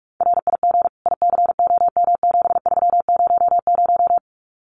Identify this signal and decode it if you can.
Morse code, heard as a keyed tone, about 44 wpm, 700 Hz. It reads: RIG ICOM7300
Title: Common QSO Phrases 45wpm